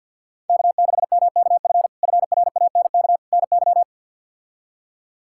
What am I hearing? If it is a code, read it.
Morse code, heard as a keyed tone, about 53 wpm, 700 Hz. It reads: K6MKF FRANK NY
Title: K1USN SST QSO Practice - 50wpm